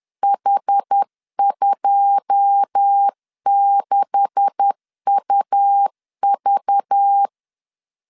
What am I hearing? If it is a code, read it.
Morse code, heard as a keyed tone, about 10 wpm, 800 Hz. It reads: H26UV
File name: LEZIONI / 9 286 10wpm 2735